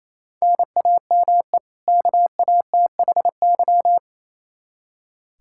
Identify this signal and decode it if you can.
Morse code, heard as a keyed tone, about 28 wpm, 700 Hz. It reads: NAME KATHY
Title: Practice QSO - Exchange 1 - Beginning - 28wpm